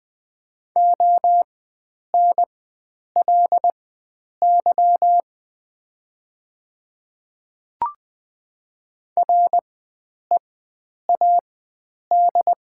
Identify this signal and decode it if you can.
Morse code, heard as a keyed tone, about 20 wpm, spaced as for 9 wpm, 690 Hz. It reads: ONLY READ
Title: 4x Character Spacing - Top 300 Words - 20wpm